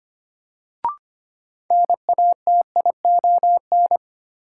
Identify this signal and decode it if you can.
Morse code, heard as a keyed tone, about 25 wpm, 700 Hz. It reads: NATION
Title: Sentences from Top 900 Words - New Word Drill - 25wpm